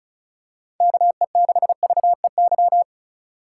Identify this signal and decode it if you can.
Morse code, heard as a keyed tone, about 36 wpm, 700 Hz. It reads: KE6VEY
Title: USA Call Signs - 6 Characters - 2x Character Spacing - 35wpm